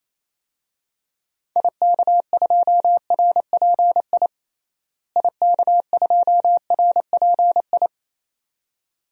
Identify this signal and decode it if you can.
Morse code, heard as a keyed tone, about 28 wpm, 700 Hz. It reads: IK2RPI IK2RPI
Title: DXCC-100 Call Signs - 6 Characters - Repeated 2x - 28wpm